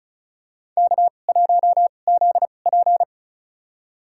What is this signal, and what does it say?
Morse code, heard as a keyed tone, about 34 wpm, spaced as for 25 wpm, 700 Hz. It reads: K1ZP
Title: USA Call Signs - 4 Characters - 2x Character Spacing - 35wpm